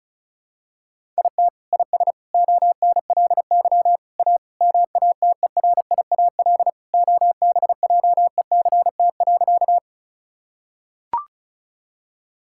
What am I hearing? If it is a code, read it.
Morse code, heard as a keyed tone, about 36 wpm, 700 Hz. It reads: IT IS ONLY A MATERIAL OBJECT.
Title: Sentences from Top 1000 Words - Review Part 5 - 35wpm